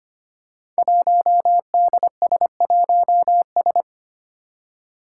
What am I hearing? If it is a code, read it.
Morse code, heard as a keyed tone, about 25 wpm, 700 Hz. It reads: 1DS1S
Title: Five Challenging Letter-Number Combinations 25wpm